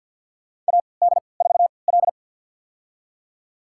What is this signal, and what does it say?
Morse code, heard as a keyed tone, about 50 wpm, spaced as for 28 wpm, 700 Hz. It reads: AD4L
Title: USA Call Signs - 3x Character Spacing - 50wpm